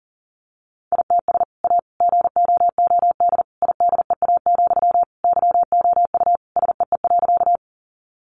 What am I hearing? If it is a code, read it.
Morse code, heard as a keyed tone, about 40 wpm, 700 Hz. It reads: ITS A GOOD IDEA, YOU SEE.
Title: Sentences from Top 600 Words 40wpm